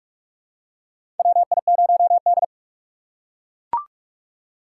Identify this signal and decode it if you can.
Morse code, heard as a keyed tone, about 42 wpm, 700 Hz. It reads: WI0D
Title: USA Call Signs - 4 Characters - 6x Character Spacing - 45wpm